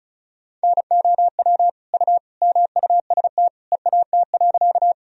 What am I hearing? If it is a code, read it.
Morse code, heard as a keyed tone, about 36 wpm, 700 Hz. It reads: NOW U MUST EAT.
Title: Sentences from Top 300 Words and Abbreviations 35wpm